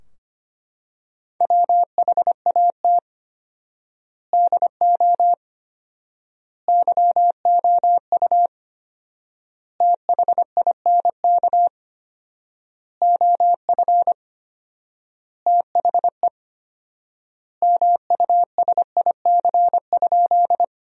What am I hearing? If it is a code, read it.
Morse code, heard as a keyed tone, about 25 wpm, 700 Hz. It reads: WHAT DO YOU THINK OF THE MUSIC?
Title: 4x Word-Spacing - Sentences from Top 300 Words - 25wpm